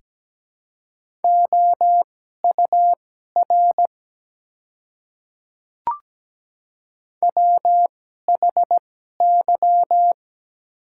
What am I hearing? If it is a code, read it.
Morse code, heard as a keyed tone, about 17 wpm, spaced as for 12 wpm, 700 Hz. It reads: OUR WHY
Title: 2x Character Spacing - Top 200 Words - 17wpm